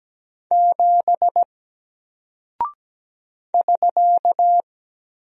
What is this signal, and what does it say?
Morse code, heard as a keyed tone, about 17 wpm, 700 Hz. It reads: 7 <SK>
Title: Single Character (Letter:Number:Punctuation:Prosign) Alt 17wpm